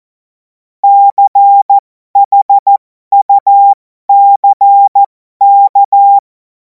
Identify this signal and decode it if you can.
Morse code, heard as a keyed tone, about 14 wpm, spaced as for 12 wpm, 800 Hz. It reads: CHUCK
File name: Morsekurs / Witze